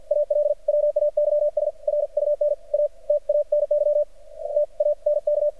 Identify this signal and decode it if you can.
Morse code, heard as a keyed tone, about 48 wpm, spaced as for 41 wpm, 590 Hz. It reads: ALWAYS RUN AT ANY TANK
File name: archived / CRSnet